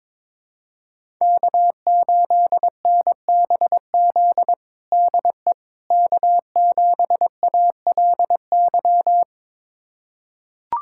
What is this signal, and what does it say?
Morse code, heard as a keyed tone, about 21 wpm, 700 Hz. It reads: K8NBZ DE K7ALY
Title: USA Callsign DE Callsign - 5 Characters - 22wpm